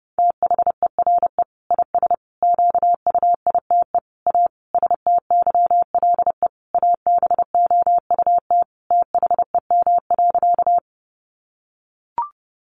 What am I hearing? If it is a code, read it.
Morse code, heard as a keyed tone, about 30 wpm, 700 Hz. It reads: THERE IS QUITE A STYLE ABOUT THEM.
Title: Sentences from Top 800 Words 30wpm